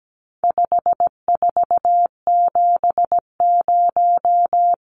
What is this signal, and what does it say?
Morse code, heard as a keyed tone, about 17 wpm, 700 Hz. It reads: 5470
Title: Numbers - 4-Digit 17wpm